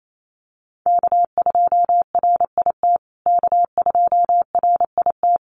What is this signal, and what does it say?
Morse code, heard as a keyed tone, about 28 wpm, 700 Hz. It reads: K2RIT K2RIT
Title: USA Call Signs Repeated 2x 28wpm